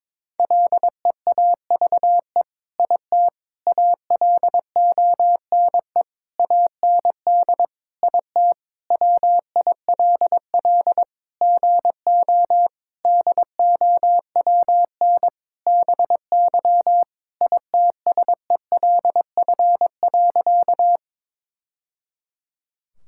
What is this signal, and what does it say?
Morse code, heard as a keyed tone, about 22 wpm, 700 Hz. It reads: LEAVE IT ALONE AND IT WILL GO DOWN BY ITSELF.
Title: Sentences from Top 1000 Words - Review Part 5 - 22wpm